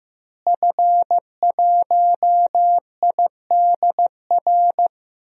Morse code, heard as a keyed tone, about 15 wpm, 700 Hz. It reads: F1IDR
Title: DXCC-100 Call Signs - 5 Characters - Repeated 3x - 15wpm